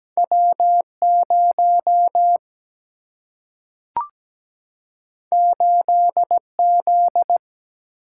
Morse code, heard as a keyed tone, about 17 wpm, 700 Hz. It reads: W0 8Z
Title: Single-Letter Single-Number 17wpm